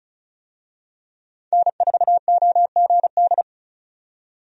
Morse code, heard as a keyed tone, about 36 wpm, 700 Hz. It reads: N4OGD
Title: USA Call Signs - 5 Characters - 35wpm